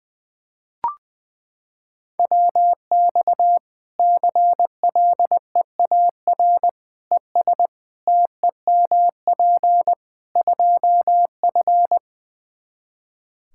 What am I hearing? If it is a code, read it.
Morse code, heard as a keyed tone, about 20 wpm, 700 Hz. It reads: WX CLEAR ES TEMP 2F
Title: Practice QSO - Exchange 2 - Info - 20wpm